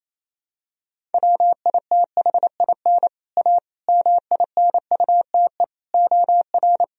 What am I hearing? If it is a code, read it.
Morse code, heard as a keyed tone, about 28 wpm, 700 Hz. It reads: WITHIN A MINUTE OR
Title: Sentences from Top 1000 Words - Review Part 4 - 28wpm